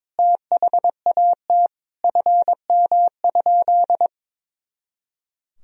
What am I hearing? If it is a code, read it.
Morse code, heard as a keyed tone, about 22 wpm, 700 Hz. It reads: THAT FM?
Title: Sentences from Top 300 Words and Abbreviations 22wpm